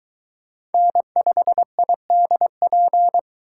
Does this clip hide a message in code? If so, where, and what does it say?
Yes: Morse code from 0.7 to 3.2 s, heard as a keyed tone, about 22 wpm, 700 Hz. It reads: N5IDP